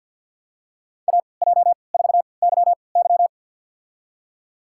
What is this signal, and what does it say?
Morse code, heard as a keyed tone, about 53 wpm, spaced as for 29 wpm, 700 Hz. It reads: AJ4YY
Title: USA Call Signs - 5 Characters - 3x Character Spacing - 50wpm